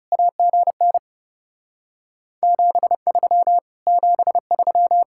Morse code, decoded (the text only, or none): AGN 73 73